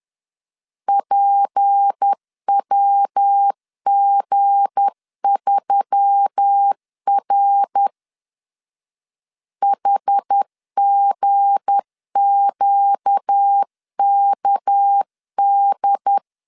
Morse code, heard as a keyed tone, about 10 wpm, 800 Hz. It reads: PWG3R HGQKD
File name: LEZIONI / 7 GF9Z / 7 G 10wpm 2501